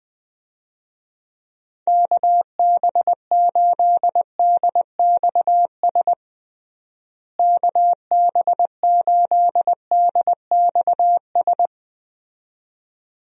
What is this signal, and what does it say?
Morse code, heard as a keyed tone, about 20 wpm, 700 Hz. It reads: KB8DXS KB8DXS
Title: USA Call Signs - 6 Characters - Repeated 2x - 20wpm